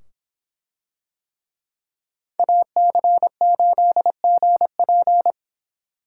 Morse code, heard as a keyed tone, about 26 wpm, 700 Hz. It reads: AC8GP